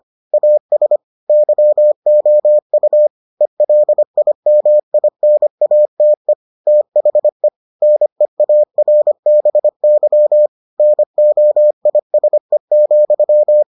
Morse code, heard as a keyed tone, about 25 wpm, 600 Hz. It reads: AS YOU ELIMINATE THE NEARBY NOISE,